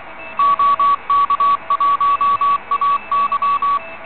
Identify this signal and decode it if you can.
Morse code, heard as a keyed tone, about 23 wpm, 1140 Hz. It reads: OK1AY
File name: OK QRP 2017 / OKQRP2017 Zvuk / REC59